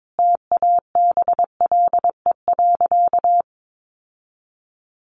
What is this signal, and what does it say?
Morse code, heard as a keyed tone, about 22 wpm, 700 Hz. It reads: TABLE.
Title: Sentences from Top 1200 Words 22wpm